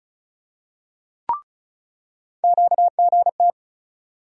Morse code, heard as a keyed tone, about 36 wpm, 700 Hz. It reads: QGT
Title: USA Call Sign Postfixes - 3 Letters 35wpm